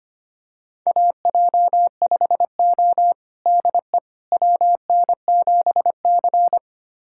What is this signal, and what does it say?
Morse code, heard as a keyed tone, about 25 wpm, 700 Hz. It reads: AJ5O DE WN7C
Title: USA Callsign DE Callsign - 4 Characters - 25wpm